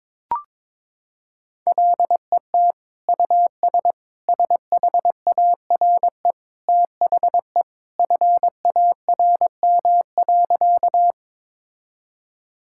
Morse code, heard as a keyed tone, about 22 wpm, 700 Hz. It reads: LET US SHARE THE FARM.
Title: Sentences from Top 1000 Words - Review Part 6 - 22wpm